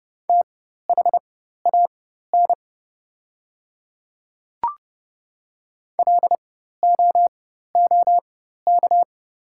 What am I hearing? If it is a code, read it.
Morse code, heard as a keyed tone, about 29 wpm, spaced as for 14 wpm, 700 Hz. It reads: THAN LOOK
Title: 4x Character Spacing - Top 100 Words - 30wpm